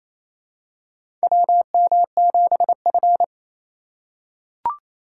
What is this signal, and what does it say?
Morse code, heard as a keyed tone, about 28 wpm, 700 Hz. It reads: WM7F
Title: USA Call Signs - 4 Characters - 3x Character Spacing - 28wpm